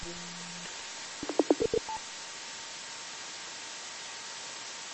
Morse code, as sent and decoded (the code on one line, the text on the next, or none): none